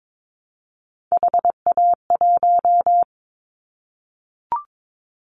Morse code, heard as a keyed tone, about 22 wpm, 700 Hz. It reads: HA1
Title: Callsign Prefixes for Top 25 DXCC - Repeated 3x - 22wpm